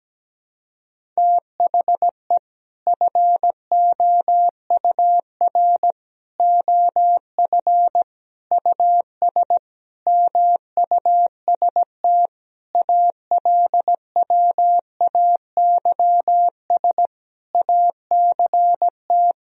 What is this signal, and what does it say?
Morse code, heard as a keyed tone, about 17 wpm, 700 Hz. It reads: THE FOUR OF US MUST ALWAYS ACT
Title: Sentences from Top 1000 Words - Review Part 1 - 17wpm